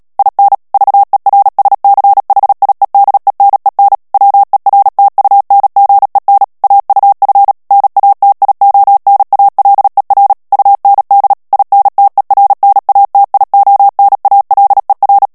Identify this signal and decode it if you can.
Morse code, heard as a keyed tone, about 37 wpm, 800 Hz. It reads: IN VERSCHIEDENEN WERTUNGEN AUF NATIONALER UND INTERNATIONALER
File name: morse mp3 / ALLEMAND texte clair 2 vitesse 35 mots minute